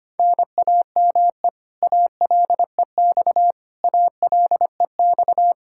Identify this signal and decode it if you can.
Morse code, heard as a keyed tone, about 25 wpm, 700 Hz. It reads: NAME ALEX ALEX
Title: Practice QSO - Exchange 1 - Beginning - 25wpm